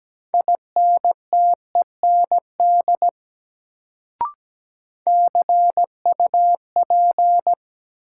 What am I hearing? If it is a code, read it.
Morse code, heard as a keyed tone, about 17 wpm, 700 Hz. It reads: INTEND CUP
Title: Sentences from Top 1600 Words - New Word Drill - 17wpm